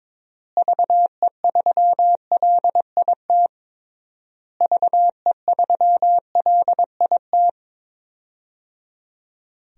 Morse code, heard as a keyed tone, about 22 wpm, 700 Hz. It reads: VE3LIT VE3LIT
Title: DXCC-100 Call Signs - 6 Characters - Repeated 3x - 22wpm